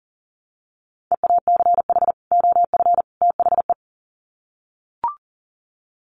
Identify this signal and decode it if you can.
Morse code, heard as a keyed tone, about 40 wpm, 700 Hz. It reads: EACH OF THE
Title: Sets of 3 Words 40wpm